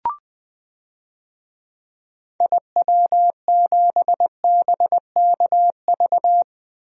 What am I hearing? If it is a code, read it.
Morse code, heard as a keyed tone, about 20 wpm, 700 Hz. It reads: IW7BKV
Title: DXCC-100 Call Signs - 6 Characters - Repeated 3x - 20wpm